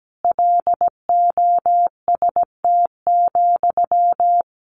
Morse code, heard as a keyed tone, about 17 wpm, 700 Hz. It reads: LOST,
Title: Sentences from Top 1500 Words 17wpm